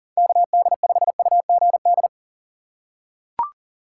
Morse code, heard as a keyed tone, about 40 wpm, 700 Hz. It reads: KD5UGD
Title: USA Call Signs - 6 Characters - 6x Character Spacing - 40wpm